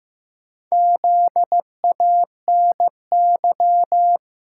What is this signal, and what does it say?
Morse code, heard as a keyed tone, about 15 wpm, 700 Hz. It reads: ZANY
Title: Pangram Challenge v2 - 15wpm